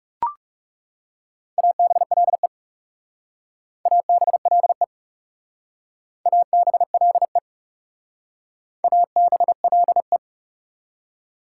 Morse code, heard as a keyed tone, about 38 wpm, 690 Hz. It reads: ABLE ABLE ABLE ABLE